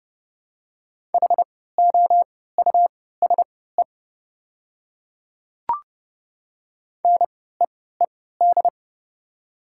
Morse code, heard as a keyed tone, about 30 wpm, spaced as for 17 wpm, 700 Hz. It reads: HOUSE NEED
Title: 3x Character Spacing - Top 500 Words - 30wpm